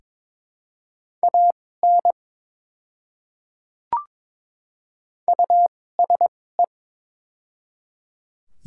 Morse code, heard as a keyed tone, about 22 wpm, spaced as for 16 wpm, 700 Hz. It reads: AN USE
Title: 2x Character Spacing - Top 100 Words - 22wpm